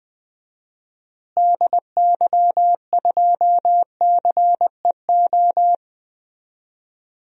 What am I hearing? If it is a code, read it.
Morse code, heard as a keyed tone, about 20 wpm, 700 Hz. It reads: DY2CEO
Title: DXCC-100 Call Signs - 6 Characters - Repeated 3x - 20wpm